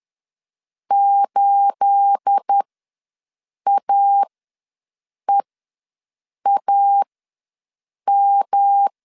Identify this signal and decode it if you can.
Morse code, heard as a keyed tone, about 10 wpm, spaced as for 6 wpm, 800 Hz. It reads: 8AEAM
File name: LEZIONI / 9 28 ws 10wpm 4124